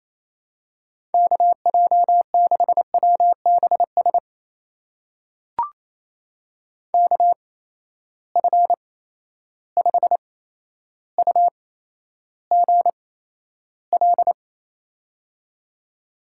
Morse code, heard as a keyed tone, about 28 wpm, spaced as for 12 wpm, 700 Hz. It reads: KJ6WBS KF5UGL
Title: USA Call Signs - 6 Characters - 8x Character Spacing - 28wpm